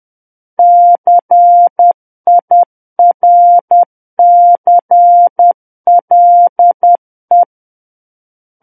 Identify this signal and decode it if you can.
Morse code, heard as a keyed tone, about 10 wpm, 700 Hz. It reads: CIRCLE
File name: Book One / 01-Chapter 1 - 10 WPM